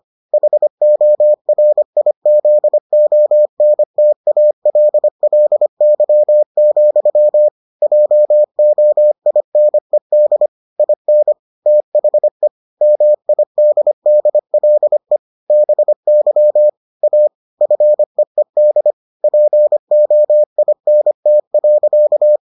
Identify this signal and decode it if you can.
Morse code, heard as a keyed tone, about 25 wpm, 600 Hz. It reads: HORIZONTALLY, JOINED IN THE MIDDLE BY A FEED POINT.